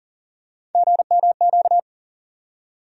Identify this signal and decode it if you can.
Morse code, heard as a keyed tone, about 40 wpm, 700 Hz. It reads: GMQ
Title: USA Call Sign Postfixes - 3 Letters 40wpm